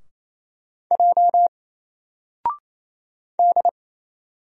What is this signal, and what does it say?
Morse code, heard as a keyed tone, about 27 wpm, 700 Hz. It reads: J D
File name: Single-Letter-Number-28wpm